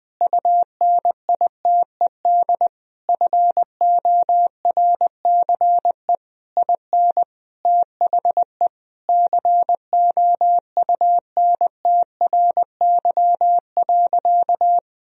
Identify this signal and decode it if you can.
Morse code, heard as a keyed tone, about 21 wpm, 700 Hz. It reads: UNITED FORCE IN THE COUNTRY.
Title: Sentences from Top 1000 Words - Review Part 4 - 20wpm